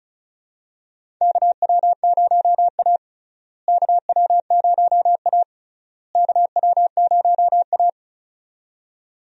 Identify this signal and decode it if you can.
Morse code, heard as a keyed tone, about 36 wpm, 700 Hz. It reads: KW0A KW0A KW0A
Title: USA Call Signs - 4 Characters - Repeated 3x - 35wpm